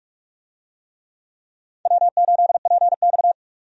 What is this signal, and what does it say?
Morse code, heard as a keyed tone, about 46 wpm, 700 Hz. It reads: W8PX
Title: USA Call Signs - 4 Characters - 45wpm